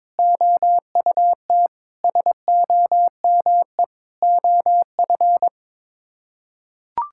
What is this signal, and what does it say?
Morse code, heard as a keyed tone, about 22 wpm, 700 Hz. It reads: OUT SOME OF
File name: Sets of 9 Words - 22wpm